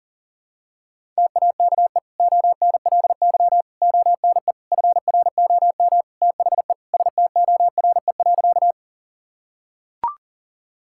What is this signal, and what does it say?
Morse code, heard as a keyed tone, about 40 wpm, 700 Hz. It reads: TAKE ONLY ONE FROM THE STORE.
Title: Sentences from Top 1000 Words - Review Part 5 - 40wpm